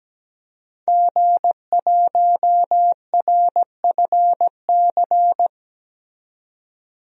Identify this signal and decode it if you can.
Morse code, heard as a keyed tone, about 17 wpm, 700 Hz. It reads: G1RFC
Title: DXCC-100 Call Signs - 5 Characters - 17wpm